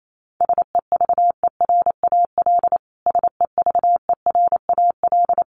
Morse code, heard as a keyed tone, about 28 wpm, 700 Hz. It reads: SEVERAL SEVERAL
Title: Top 500 Words Repeated 28wpm